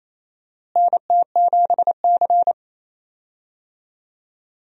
Morse code, heard as a keyed tone, about 28 wpm, 700 Hz. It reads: NT7C
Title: USA Call Signs - 4 Characters - Repeated 3x - 28wpm